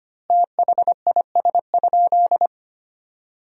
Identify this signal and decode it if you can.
Morse code, heard as a keyed tone, about 25 wpm, 700 Hz. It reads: THIS?
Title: 3x Word-Spacing - Sentences from Top 100 Words - 25wpm